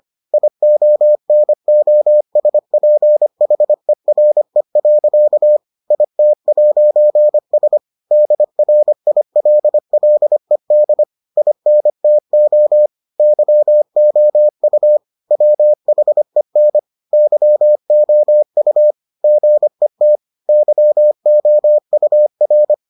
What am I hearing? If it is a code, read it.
Morse code, heard as a keyed tone, about 25 wpm, 600 Hz. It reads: IONOSPHERE. IT'S DRILLED INTO YOU WHEN YOU GET YOUR